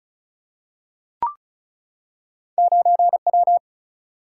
Morse code, heard as a keyed tone, about 35 wpm, 700 Hz. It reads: 9 W
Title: Single-Letter Single-Number 35wpm